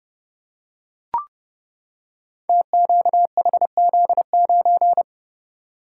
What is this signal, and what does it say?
Morse code, heard as a keyed tone, about 30 wpm, 700 Hz. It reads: TQHZ9
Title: Code Groups - 5 Letter:Numbers - 30wpm